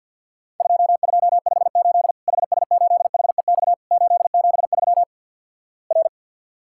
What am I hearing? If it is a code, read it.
Morse code, heard as a keyed tone, about 50 wpm, 700 Hz. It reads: 2258 HS8HEX 873 R
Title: Worked All Europe DX Contest - QTC Exchange with Repeats - 50wpm